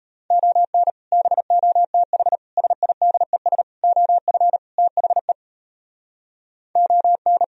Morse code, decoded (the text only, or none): ON BOTH SIDES OF THE ON